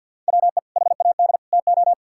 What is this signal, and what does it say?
Morse code, heard as a keyed tone, about 50 wpm, 700 Hz. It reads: WE HAD TO
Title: Sentences from Top 1600 Words 50wpm